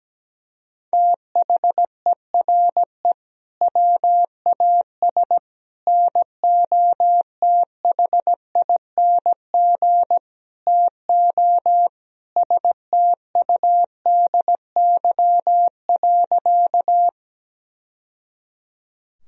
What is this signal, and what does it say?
Morse code, heard as a keyed tone, about 17 wpm, 700 Hz. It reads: THERE WAS NOTHING TO STUDY.
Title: Sentences from Top 1000 Words - Review Part 1 - 17wpm